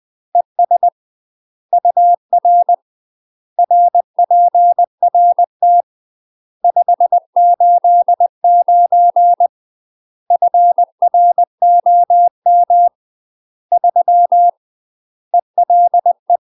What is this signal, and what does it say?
Morse code, heard as a keyed tone, about 20 wpm, 700 Hz. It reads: ES UR RPRT 589 FROM 3 ELE
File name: UK MASTER QSOs TRAINING / 20WPM / UK MASTER QSOs 0032